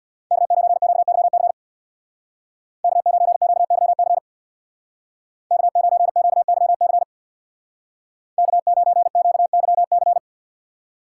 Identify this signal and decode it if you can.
Morse code, heard as a keyed tone, about 61 wpm, 700 Hz. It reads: K9QYC K9QYC K9QYC K9QYC
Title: Speed Racing - USA 1x3 Call Signs 50wpm